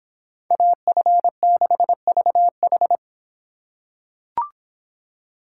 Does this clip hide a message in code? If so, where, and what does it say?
Yes: Morse code from 0.5 to 3.0 s, heard as a keyed tone, about 24 wpm, 700 Hz. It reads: AF6VH